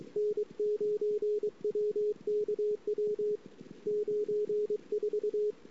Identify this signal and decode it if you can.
Morse code, heard as a keyed tone, about 23 wpm, 420 Hz. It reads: N9WKW 94
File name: ss-cw